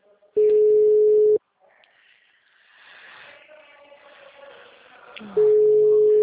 Ringback tone (Europe, 425 Hz).